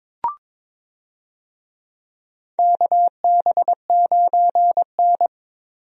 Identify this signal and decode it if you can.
Morse code, heard as a keyed tone, about 22 wpm, 700 Hz. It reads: KB9N
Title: USA Call Signs - 4 Characters - Repeated 2x - 22wpm